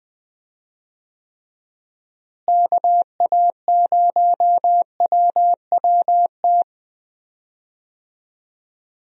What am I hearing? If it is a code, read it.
Morse code, heard as a keyed tone, about 20 wpm, 700 Hz. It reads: KA0WWT
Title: USA Call Signs - 6 Characters - Repeated 3x - 20wpm